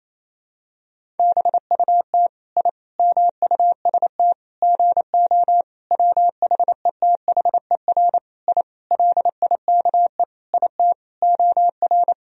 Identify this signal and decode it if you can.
Morse code, heard as a keyed tone, about 28 wpm, 700 Hz. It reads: BUT I MUST GO WHETHER I LIKE IT OR
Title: Sentences from Top 1000 Words - Review Part 3 - 28wpm